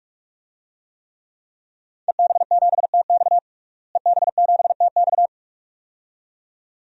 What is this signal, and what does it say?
Morse code, heard as a keyed tone, about 42 wpm, 700 Hz. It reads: EB7TX EB7TX
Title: DXCC-100 Call Signs - 5 Characters - Repeated 2x - 45wpm